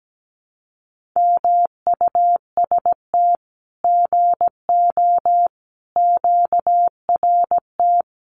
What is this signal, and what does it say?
Morse code, heard as a keyed tone, about 17 wpm, 700 Hz. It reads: MUST GO QRT
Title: Common QSO Phrases 17wpm